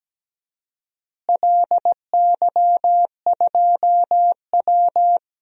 Morse code, heard as a keyed tone, about 17 wpm, 700 Hz. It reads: LY2W
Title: DXCC-100 Call Signs - 4 Characters - Repeated 2x - 17wpm